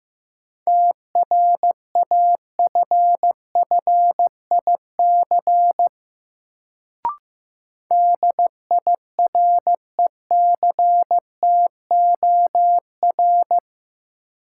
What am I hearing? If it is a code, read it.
Morse code, heard as a keyed tone, about 15 wpm, 700 Hz. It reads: TRAFFIC DIRECTOR
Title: Sentences from Top 1300 Words - New Word Drill Alt - 15wpm